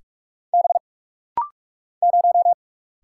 Morse code, heard as a keyed tone, about 44 wpm, 700 Hz. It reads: B 0
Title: Single Letter-Number - Rapid-Fire 45wpm